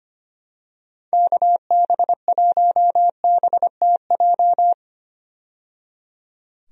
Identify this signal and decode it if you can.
Morse code, heard as a keyed tone, about 25 wpm, 700 Hz. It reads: KB1BTJ
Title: USA Call Signs - 6 Characters - 25wpm